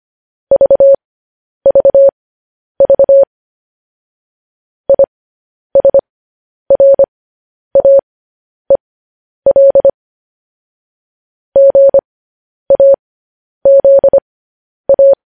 Morse code, heard as a keyed tone, about 25 wpm, spaced as for 10 wpm, 550 Hz. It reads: VVV ISRAEL GAZA
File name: podcast / News headlines-1x-25x10